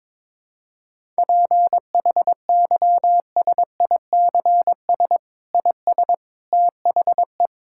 Morse code, heard as a keyed tone, about 22 wpm, 700 Hz. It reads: PHYSICS IS THE
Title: Sentences from Top 1000 Words - Review Part 5 - 22wpm